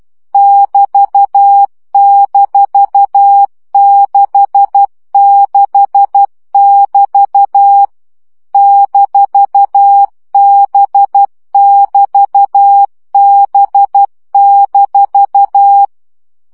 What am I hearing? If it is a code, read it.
Morse code, heard as a keyed tone, about 12 wpm, 800 Hz. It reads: <BT>-66<BT> -B<BT>B-
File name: CW gyakorlas b 6 egyenlo minusz 10 baud